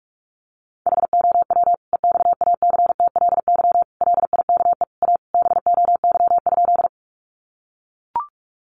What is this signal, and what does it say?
Morse code, heard as a keyed tone, about 44 wpm, 700 Hz. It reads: HOW EXACTLY LIKE A BOY?
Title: Sentences from Top 900 Words 45wpm